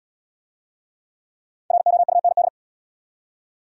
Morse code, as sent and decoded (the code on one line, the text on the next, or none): .-. --- ..- -. -..
ROUND